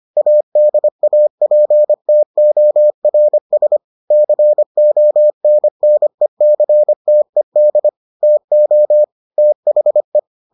Morse code, heard as a keyed tone, about 25 wpm, 600 Hz. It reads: ADAPTORS CONNECTED TO THE